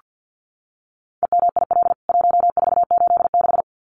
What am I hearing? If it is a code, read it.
Morse code, heard as a keyed tone, about 48 wpm, 700 Hz. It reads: EMIL 1486
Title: CWT Exchange 50wpm